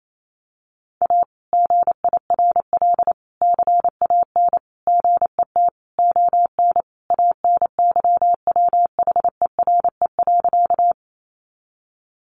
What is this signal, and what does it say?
Morse code, heard as a keyed tone, about 28 wpm, 700 Hz. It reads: A GIRL CAN GET ON ANYWHERE.
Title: Sentences from Top 1400 Words 28wpm